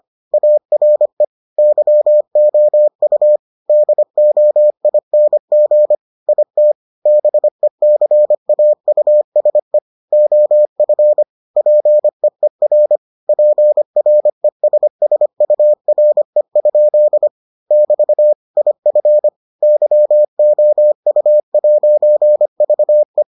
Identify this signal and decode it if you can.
Morse code, heard as a keyed tone, about 25 wpm, 600 Hz. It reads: ARE YOU DOING IT BECAUSE OF PEER PRESSURE? <BT> IF YOU'VE